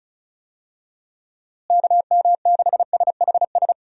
Morse code, heard as a keyed tone, about 34 wpm, 700 Hz. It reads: KM6SHS
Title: USA Call Signs - 6 Characters - Repeated 2x - 35wpm